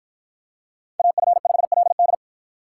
Morse code, heard as a keyed tone, about 53 wpm, 700 Hz. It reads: AF5LD